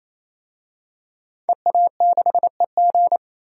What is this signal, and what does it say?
Morse code, heard as a keyed tone, about 28 wpm, 700 Hz. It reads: EA6EG
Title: DXCC-100 Call Signs - 5 Characters - 28wpm